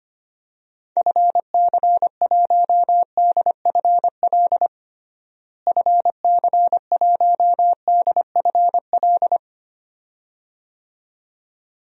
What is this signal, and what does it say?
Morse code, heard as a keyed tone, about 25 wpm, 700 Hz. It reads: FC1DFL FC1DFL
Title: DXCC-100 Call Signs - 6 Characters - Repeated 2x - 25wpm